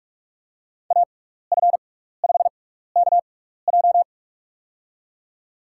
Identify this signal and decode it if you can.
Morse code, heard as a keyed tone, about 42 wpm, spaced as for 15 wpm, 700 Hz. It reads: AF5KJ
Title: USA Call Signs - 5 Characters - 6x Character Spacing - 45wpm